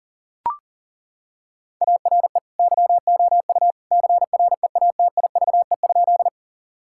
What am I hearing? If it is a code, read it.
Morse code, heard as a keyed tone, about 40 wpm, 700 Hz. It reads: ARE YOU CREATIVE?
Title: Sentences from Top 1000 Words - Review Part 5 - 40wpm